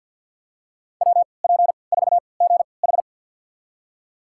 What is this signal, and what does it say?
Morse code, heard as a keyed tone, about 51 wpm, spaced as for 28 wpm, 700 Hz. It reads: WP4GH
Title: USA Call Signs - 5 Characters - 3x Character Spacing - 50wpm